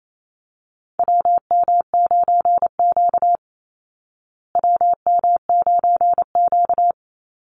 Morse code, heard as a keyed tone, about 28 wpm, 700 Hz. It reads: WM9Q WM9Q
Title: Call Signs Repeated 3x - 28wpm